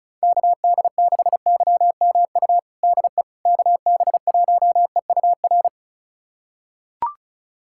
Morse code, heard as a keyed tone, about 36 wpm, 700 Hz. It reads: KD6YMU DE KB1EUR
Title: USA Callsign DE Callsign - 6 Characters - 35wpm